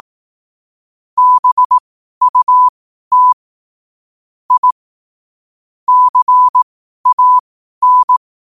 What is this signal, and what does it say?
Morse code, heard as a keyed tone, about 18 wpm, spaced as for 12 wpm, 1000 Hz. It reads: BUT I CAN
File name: qotdfiles / 12wpm